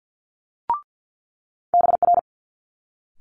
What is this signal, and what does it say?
Morse code, heard as a keyed tone, about 50 wpm, 700 Hz. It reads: BR
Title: Multi-letter Phonemes Quick Fire 50wpm